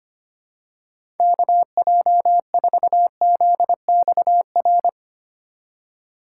Morse code, heard as a keyed tone, about 25 wpm, 700 Hz. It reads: KJ4ZXR
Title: USA Call Signs - 6 Characters - 2x Character Spacing - 25wpm